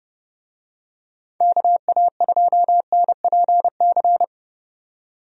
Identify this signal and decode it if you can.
Morse code, heard as a keyed tone, about 30 wpm, 700 Hz. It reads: KA2NPC
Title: USA Call Signs - 6 Characters - 30wpm